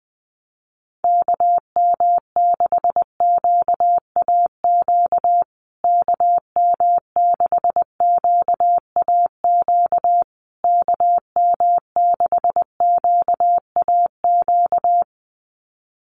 Morse code, heard as a keyed tone, about 20 wpm, 700 Hz. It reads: KM6QAQ KM6QAQ KM6QAQ
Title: USA Call Signs Repeated 3x 20wpm